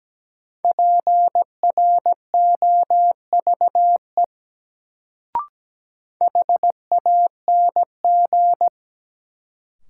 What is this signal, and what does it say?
Morse code, heard as a keyed tone, about 17 wpm, 700 Hz. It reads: PROVE HANG
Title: Sentences from Top 1600 Words - New Word Drill - 17wpm